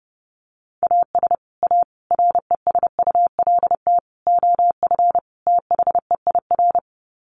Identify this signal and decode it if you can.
Morse code, heard as a keyed tone, about 30 wpm, 700 Hz. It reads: AS A RESULT OF THEIR
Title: Sets of 5 Words Encore2 30wpm